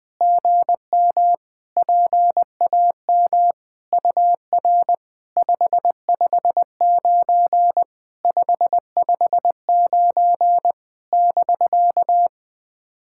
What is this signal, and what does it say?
Morse code, heard as a keyed tone, about 20 wpm, 700 Hz. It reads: GM PAM UR 559 559 <BK>
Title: SOTA QSO Practice 20wpm